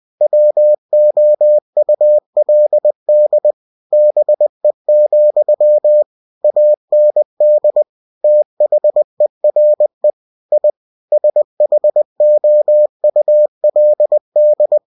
Morse code, heard as a keyed tone, about 20 wpm, 600 Hz. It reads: WOULD BE, AND THERE I SHOULD
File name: cw-books / winnie-the-pooh / winnie-20wpm-page-0037